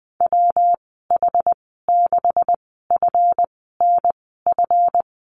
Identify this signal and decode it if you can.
Morse code, heard as a keyed tone, about 19 wpm, spaced as for 14 wpm, 700 Hz. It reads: WH6FNF
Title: 154 Lesson 25g - Letter p - Call Signs 20wpm